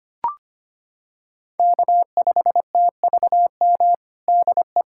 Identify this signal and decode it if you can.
Morse code, heard as a keyed tone, about 25 wpm, 700 Hz. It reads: K5TVM DE
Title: USA Callsign DE Callsign 25wpm